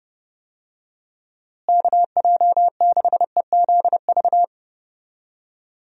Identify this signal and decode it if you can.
Morse code, heard as a keyed tone, about 30 wpm, 700 Hz. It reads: KJ6EZV
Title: USA Call Signs - 6 Characters - Repeated 2x - 30wpm